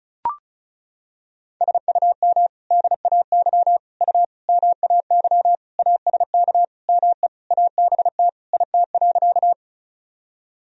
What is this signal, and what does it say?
Morse code, heard as a keyed tone, about 34 wpm, 700 Hz. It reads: SUM DAY U MAY ASK ME ABT IT.
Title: Sentences from Top 200 Words and Abbreviations 35wpm